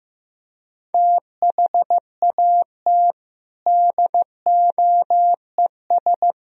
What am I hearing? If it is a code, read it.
Morse code, heard as a keyed tone, about 15 wpm, 700 Hz. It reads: THAT DOES
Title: Speed-Racing - Sets of 3 Words 15wpm